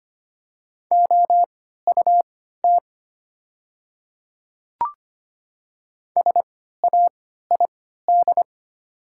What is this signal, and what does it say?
Morse code, heard as a keyed tone, about 25 wpm, spaced as for 14 wpm, 700 Hz. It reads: OUT SAID
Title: 3x Character Spacing - Top 100 Words - 25wpm